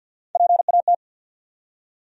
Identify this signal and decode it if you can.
Morse code, heard as a keyed tone, about 48 wpm, 700 Hz. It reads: PAT
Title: USA Common Names - Max 4 Letters - 50wpm